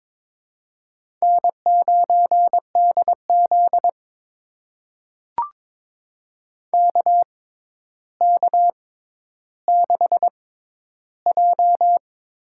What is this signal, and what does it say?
Morse code, heard as a keyed tone, about 21 wpm, spaced as for 11 wpm, 700 Hz. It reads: N9DZ KK6J
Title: USA Call Signs - 6x Character Spacing - 22wpm